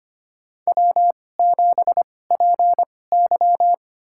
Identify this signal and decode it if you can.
Morse code, heard as a keyed tone, about 25 wpm, spaced as for 18 wpm, 700 Hz. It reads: W7PY
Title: USA Call Signs - 4 Characters - 2x Character Spacing - 25wpm